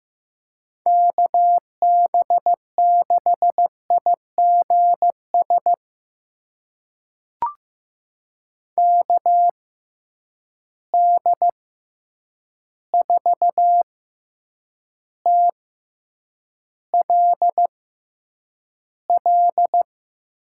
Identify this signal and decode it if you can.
Morse code, heard as a keyed tone, about 15 wpm, spaced as for 8 wpm, 700 Hz. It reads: KB6IGS KD4TLL
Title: USA Call Signs - 6 Characters - 6x Character Spacing - 15wpm